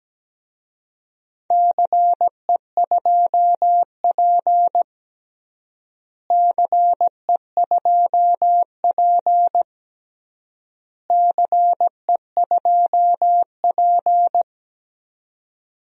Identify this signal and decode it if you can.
Morse code, heard as a keyed tone, about 17 wpm, 700 Hz. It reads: CE2P CE2P CE2P
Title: DXCC-100 Call Signs - 4 Characters - Repeated 3x - 17wpm